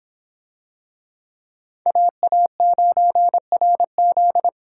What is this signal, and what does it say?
Morse code, heard as a keyed tone, about 22 wpm, 700 Hz. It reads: AA9RZ